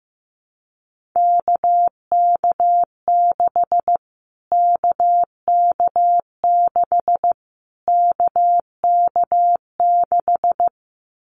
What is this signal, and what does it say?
Morse code, heard as a keyed tone, about 15 wpm, 700 Hz. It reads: KK6 KK6 KK6
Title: Callsign Prefixes for Top 25 DXCC - Repeated 3x - 15wpm